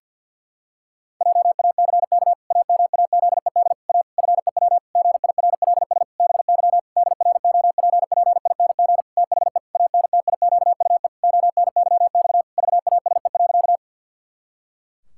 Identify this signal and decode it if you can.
Morse code, heard as a keyed tone, about 50 wpm, 700 Hz. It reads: JACK AMAZED A FEW GIRLS BY DROPPING THE ANTIQUE ONYX VASE.
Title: Pangram Challenge v2 - 50wpm